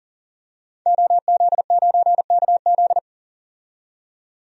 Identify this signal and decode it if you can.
Morse code, heard as a keyed tone, about 40 wpm, 700 Hz. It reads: OZ9KZ
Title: DXCC-100 Call Signs - 5 Characters - 40wpm